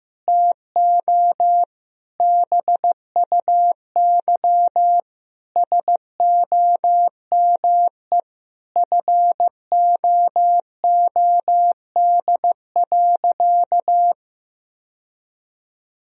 Morse code, heard as a keyed tone, about 16 wpm, 700 Hz. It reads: TO BUY SOME FOOD.
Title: Sentences from Top 1000 Words - Review Part 4 - 15wpm